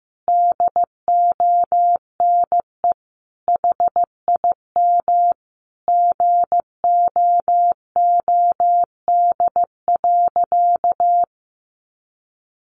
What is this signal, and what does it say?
Morse code, heard as a keyed tone, about 15 wpm, 700 Hz. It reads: DONE HIM GOOD.
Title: Sentences from Top 1200 Words 15wpm